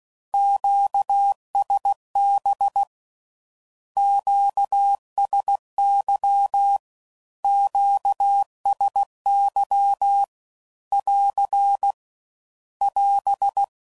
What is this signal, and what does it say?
Morse code, heard as a keyed tone, about 16 wpm, 800 Hz. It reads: QSB QSY QSY <AR> <AS>
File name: CW Code Groups @15WPM / Q Calls / Q Calls Twice